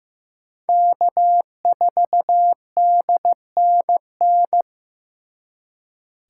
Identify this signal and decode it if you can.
Morse code, heard as a keyed tone, about 18 wpm, 700 Hz. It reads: K4DNN
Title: Speed Racing - USA 1x3 Call Signs 15wpm